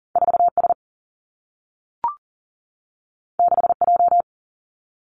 Morse code, heard as a keyed tone, about 40 wpm, 700 Hz. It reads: 4 S 6 J
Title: Two Challenging Letter-Number Combinations 40wpm